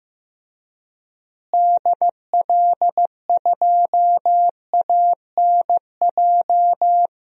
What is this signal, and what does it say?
Morse code, heard as a keyed tone, about 15 wpm, 700 Hz. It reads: DL2ANJ
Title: DXCC-100 Call Signs - 6 Characters - Repeated 3x - 15wpm